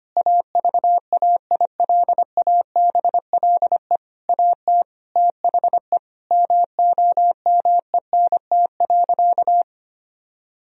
Morse code, heard as a keyed tone, about 25 wpm, 700 Hz. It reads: AVAILABLE AT THE MOMENT.
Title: Sentences from Top 1000 Words - Review Part 3 - 25wpm